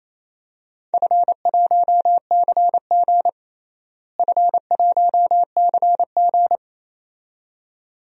Morse code, heard as a keyed tone, about 27 wpm, 700 Hz. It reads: F1CG F1CG
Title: DXCC-100 Call Signs - 4 Characters - Repeated 2x - 28wpm